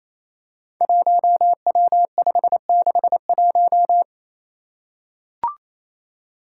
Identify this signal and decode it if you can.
Morse code, heard as a keyed tone, about 28 wpm, 700 Hz. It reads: 1W561
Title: Five Challenging Letter-Number Combinations 28wpm